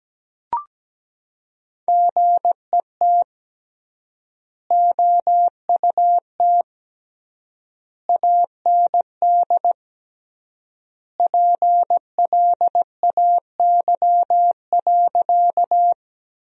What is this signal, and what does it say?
Morse code, heard as a keyed tone, about 17 wpm, 700 Hz. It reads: GET OUT AND PLAY.
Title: 3x Word-Spacing - Sentences from Top 200 Words - 17wpm